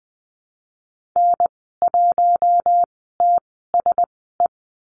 Morse code, heard as a keyed tone, about 20 wpm, spaced as for 14 wpm, 700 Hz. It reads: N1TSE
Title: 082 Lesson 14g - Number 2 - Call Signs 20wpm